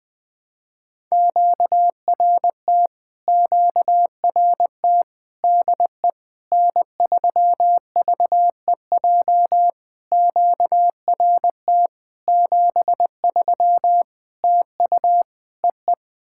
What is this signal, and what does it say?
Morse code, heard as a keyed tone, about 20 wpm, 700 Hz. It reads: QRT QRT DE N3VEJ QRT 73 TU EE
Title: POTA QSO Practice - 20wpm